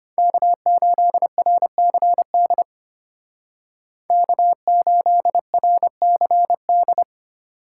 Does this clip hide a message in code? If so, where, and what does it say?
Yes: Morse code from 0.2 to 7.0 s, heard as a keyed tone, about 33 wpm, 700 Hz. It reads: K8RCD K8RCD